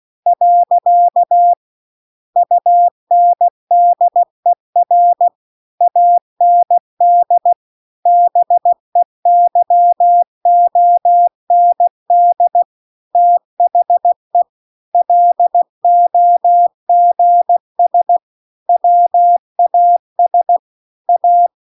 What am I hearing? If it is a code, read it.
Morse code, heard as a keyed tone, about 16 wpm, 700 Hz. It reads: . UNDER AND BEYOND THE LOGS WAS A
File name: CW-The-Big-Twoheaded-River-2-16WPM